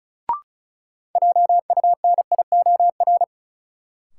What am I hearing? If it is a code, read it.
Morse code, heard as a keyed tone, about 34 wpm, 700 Hz. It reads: JUNIOR
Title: Sentences from Top 1500 Words - New Word Drill Alt - 35wpm